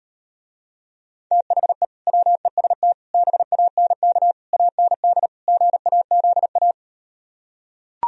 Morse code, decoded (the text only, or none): THE WEST BANK AND GAZA